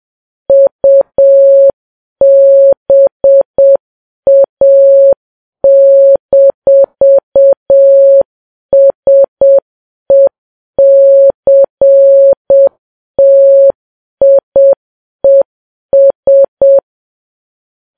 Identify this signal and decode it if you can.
Morse code, heard as a keyed tone, about 7 wpm, 550 Hz. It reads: UBA-SECTIES
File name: Files Web tls / Morse on9fv / morse7kps